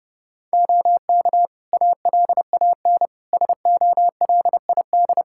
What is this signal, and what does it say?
Morse code, heard as a keyed tone, about 30 wpm, 700 Hz. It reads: OK ALAN SOLID
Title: Practice QSO - Exchange 3 - More Info - 30wpm